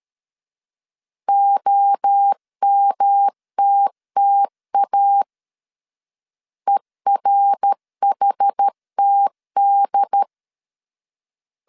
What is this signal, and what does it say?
Morse code, heard as a keyed tone, about 13 wpm, 800 Hz. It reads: OMTTA ERHTD
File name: CW / DET / 1 3 1500C S C 12wpm 2750